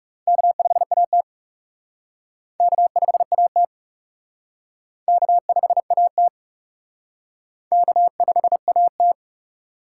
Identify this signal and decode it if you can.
Morse code, heard as a keyed tone, about 39 wpm, 700 Hz. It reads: K5AT K5AT K5AT K5AT